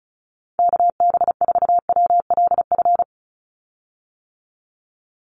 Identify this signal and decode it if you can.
Morse code, heard as a keyed tone, about 36 wpm, 700 Hz. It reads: KB4WLF
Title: Call Signs - 35wpm